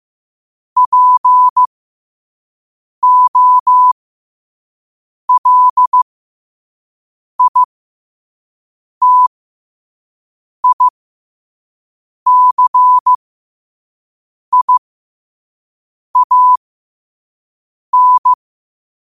Morse code, heard as a keyed tone, about 15 wpm, spaced as for 5 wpm, 1000 Hz. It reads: POLITICIAN
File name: qotdfiles / 5wpm